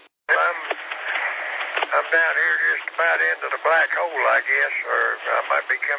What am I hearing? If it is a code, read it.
Emergency Alert System (SAME) tones, heard as harsh buzzing bursts, with speech. It is heard from 0.8 to 4.8 s.